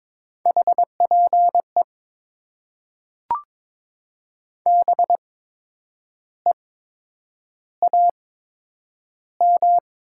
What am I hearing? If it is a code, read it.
Morse code, heard as a keyed tone, about 22 wpm, spaced as for 8 wpm, 700 Hz. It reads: HPE BEAM
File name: Top 90 QSO Elements - 8x Character Spacing - 22wpm